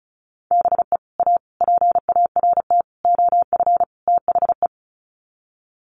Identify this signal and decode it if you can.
Morse code, heard as a keyed tone, about 34 wpm, 700 Hz. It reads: BE A PART OF THE
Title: Sets of 5 Words Encore 35wpm